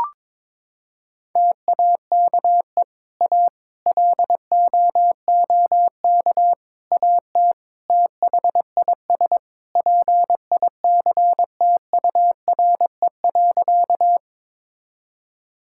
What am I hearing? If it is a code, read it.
Morse code, heard as a keyed tone, about 22 wpm, 700 Hz. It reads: TAKE A LOOK AT THIS PICTURE.
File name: practice / 22wpm / Sentences with 4-Word Phrases - 22wpm